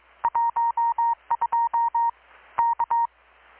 Morse code, heard as a keyed tone, about 22 wpm, 930 Hz. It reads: 12 K
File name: blog / swl / 6-abril-2019-12695-USB-KPS